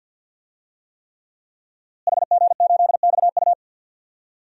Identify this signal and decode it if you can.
Morse code, heard as a keyed tone, about 51 wpm, 700 Hz. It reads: HG8XU
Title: DXCC-100 Call Signs - 5 Characters - Repeated 3x - 50wpm